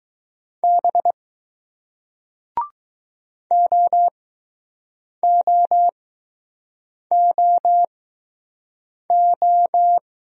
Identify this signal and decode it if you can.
Morse code, heard as a keyed tone, about 20 wpm, spaced as for 6 wpm, 700 Hz. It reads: B OOOO